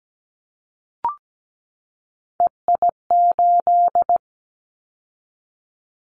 Morse code, heard as a keyed tone, about 17 wpm, 700 Hz. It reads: EI8
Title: Callsign Prefixes for Top 50 DXCC - 17wpm